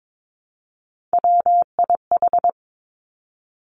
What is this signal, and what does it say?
Morse code, heard as a keyed tone, about 22 wpm, 700 Hz. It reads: WIH
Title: Three Challenging Letter-Number Combinations 22wpm